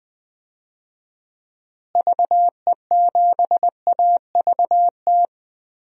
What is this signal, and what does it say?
Morse code, heard as a keyed tone, about 20 wpm, 700 Hz. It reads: VE7AVT
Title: DXCC-100 Call Signs - 6 Characters - Repeated 3x - 20wpm